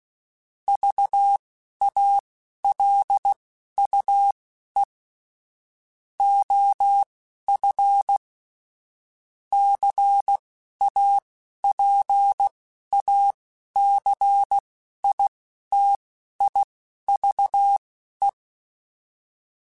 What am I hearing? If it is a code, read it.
Morse code, heard as a keyed tone, about 16 wpm, spaced as for 11 wpm, 800 Hz. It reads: VALUE OF CAPACITIVE
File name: Technical Subjects / Technical Subjects-mp3s / XC @10